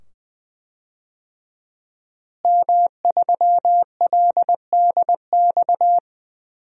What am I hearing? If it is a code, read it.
Morse code, heard as a keyed tone, about 20 wpm, 700 Hz. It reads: M3LDX
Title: DXCC-100 Call Signs - 5 Characters - Repeated 2x - 20wpm